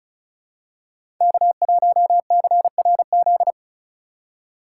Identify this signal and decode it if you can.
Morse code, heard as a keyed tone, about 36 wpm, 690 Hz. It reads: K1CRZ
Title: USA Call Signs - 5 Characters - 3x Character Spacing - 35wpm